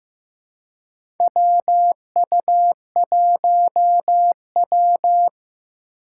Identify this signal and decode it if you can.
Morse code, heard as a keyed tone, about 15 wpm, 700 Hz. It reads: WU1W
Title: USA Call Signs - 4 Characters - 4x Character Spacing - 15wpm